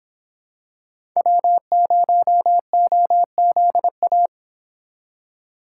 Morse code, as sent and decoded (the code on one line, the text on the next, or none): .-- ----- --- --.. .-
W0OZA